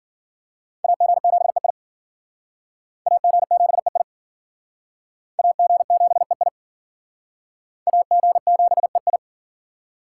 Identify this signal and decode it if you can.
Morse code, heard as a keyed tone, about 48 wpm, 700 Hz. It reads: AG7EI AG7EI AG7EI AG7EI
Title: Speed Racing - USA 2x2 Call Signs 40wpm